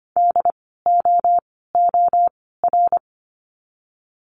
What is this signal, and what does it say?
Morse code, heard as a keyed tone, about 25 wpm, 700 Hz. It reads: D O O R
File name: course / 25:10wpm / 094 Lesson 16f - Letter c - Words 25wpm